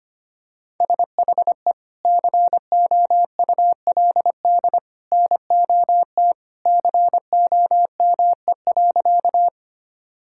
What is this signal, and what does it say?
Morse code, heard as a keyed tone, about 25 wpm, 700 Hz. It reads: SHE COULD NOT COME.
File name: practice / 25wpm / Sentences with 2-Word Phrases - 25wpm